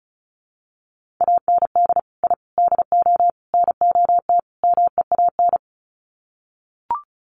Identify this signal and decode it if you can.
Morse code, heard as a keyed tone, about 34 wpm, 700 Hz. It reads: AND I DO NOT MEAN
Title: Sets of 5 Words Encore2 35wpm